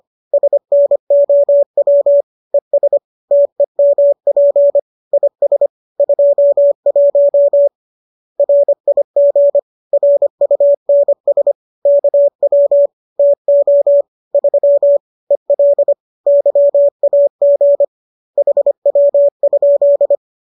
Morse code, heard as a keyed tone, about 25 wpm, 570 Hz. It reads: SNOW ES TEMP IS 21 RIG RUNS KW TO 3 EL YAG HW?
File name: qso221 25